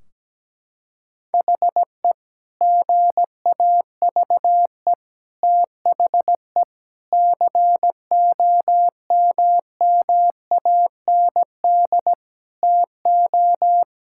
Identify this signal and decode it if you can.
Morse code, heard as a keyed tone, about 17 wpm, 700 Hz. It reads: HE GAVE THE COMMAND TO
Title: Sentences from Top 1600 Words 17wpm